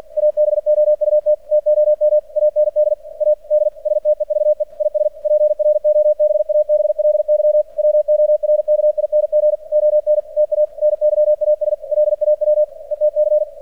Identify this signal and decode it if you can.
Morse code, heard as a keyed tone, about 50 wpm, 600 Hz. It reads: ABOUT TOM AND A D STEVE IS PROBABLY WORKING ON TA NYAS LAW ETC
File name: archived / CRSnet